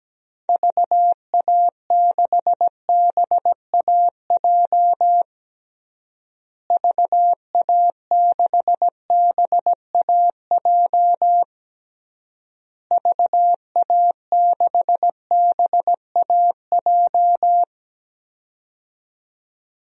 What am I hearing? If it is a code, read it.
Morse code, heard as a keyed tone, about 17 wpm, 700 Hz. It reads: VA6BAJ VA6BAJ VA6BAJ
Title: DXCC-100 Call Signs - 6 Characters - Repeated 3x - 17wpm